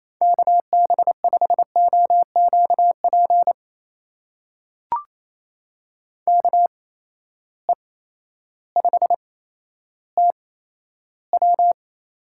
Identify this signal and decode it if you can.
Morse code, heard as a keyed tone, about 28 wpm, 700 Hz. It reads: K B 5 O Q P K E 5 T W
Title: USA Call Signs - 6 Characters - 8x Character Spacing - 28wpm